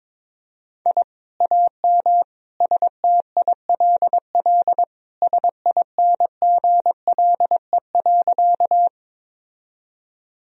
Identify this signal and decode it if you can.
Morse code, heard as a keyed tone, about 22 wpm, 700 Hz. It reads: I AM STILL SINGLE.
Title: Sentences from Top 1000 Words - Review Part 4 - 22wpm